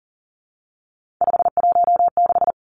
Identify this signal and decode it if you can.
Morse code, heard as a keyed tone, about 40 wpm, 700 Hz. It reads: 516
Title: Numbers - 3-Digit 40wpm